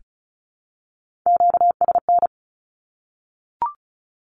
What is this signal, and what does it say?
Morse code, heard as a keyed tone, about 34 wpm, 700 Hz. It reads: QSN
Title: Q-Codes 35wpm